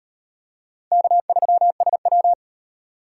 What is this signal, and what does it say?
Morse code, heard as a keyed tone, about 38 wpm, 700 Hz. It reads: K3SW